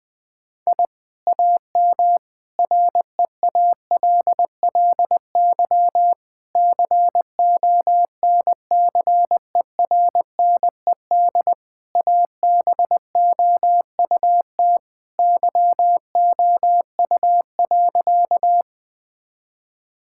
Morse code, heard as a keyed tone, about 20 wpm, 700 Hz. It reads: I AM REALLY CONCERNED ABOUT YOU.
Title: Sentences from Top 1000 Words - Review Part 6 - 20wpm